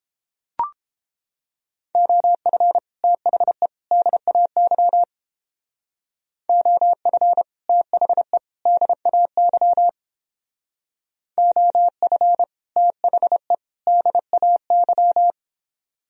Morse code, heard as a keyed tone, about 31 wpm, 700 Hz. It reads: OF THE DAY OF THE DAY OF THE DAY